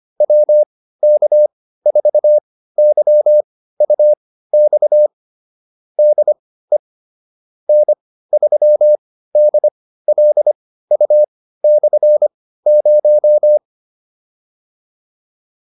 Morse code, heard as a keyed tone, about 25 wpm, spaced as for 15 wpm, 600 Hz. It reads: WK4YUX DE N3DLU/0